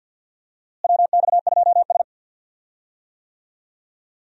Morse code, heard as a keyed tone, about 51 wpm, 700 Hz. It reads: WX2S
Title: USA Call Signs - 4 Characters - Repeated 3x - 50wpm